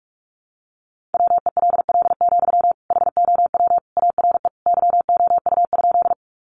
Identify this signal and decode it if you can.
Morse code, heard as a keyed tone, about 44 wpm, 700 Hz. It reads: WELL, HOW ARE YOU?
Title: Sentences from Top 200 Words 45wpm